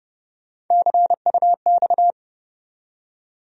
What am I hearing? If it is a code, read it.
Morse code, heard as a keyed tone, about 30 wpm, 700 Hz. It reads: CUX
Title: USA Call Sign Postfixes - 3 Letters 30wpm